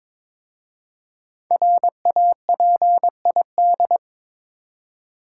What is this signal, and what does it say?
Morse code, heard as a keyed tone, about 22 wpm, 700 Hz. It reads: RAPID
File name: 5 Letter Words - First 3 Letters Given - 22wpm